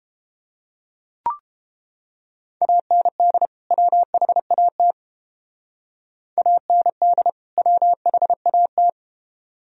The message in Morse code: AND WHAT AND WHAT

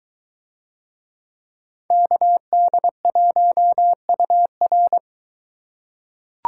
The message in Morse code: KD1UR

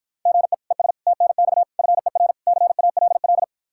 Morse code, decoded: GE ES TNX FER CALL

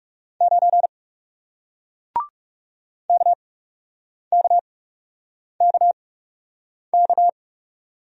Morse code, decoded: 9 KKKK

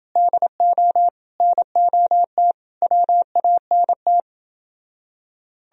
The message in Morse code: DO NOT WANT

